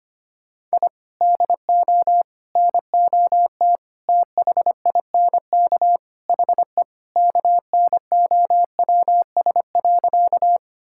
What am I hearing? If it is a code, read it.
Morse code, heard as a keyed tone, about 25 wpm, 700 Hz. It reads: I DO NOT THINK HE KNOWS.
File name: practice / 25wpm / Sentences with 4-Word Phrases - 25wpm